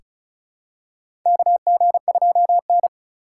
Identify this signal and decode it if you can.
Morse code, heard as a keyed tone, about 36 wpm, 700 Hz. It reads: KG2N
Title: USA Call Signs - 4x Character Spacing - 35wpm